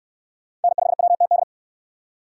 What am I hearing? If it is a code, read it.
Morse code, heard as a keyed tone, about 60 wpm, 700 Hz. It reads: N5WTD